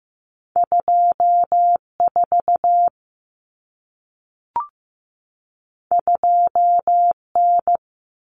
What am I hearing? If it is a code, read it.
Morse code, heard as a keyed tone, about 15 wpm, 700 Hz. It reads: 24 2N
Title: Common Cut Numbers - 2-Digit 15wpm